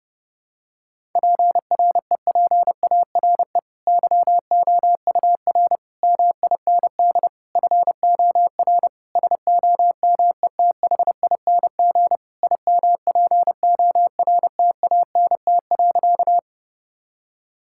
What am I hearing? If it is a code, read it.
Morse code, heard as a keyed tone, about 32 wpm, 700 Hz. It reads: PREPARE YOUR MIND FOR SOMETHING IMPORTANT.
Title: Sentences from Top 1000 Words - Review Part 6 - 30wpm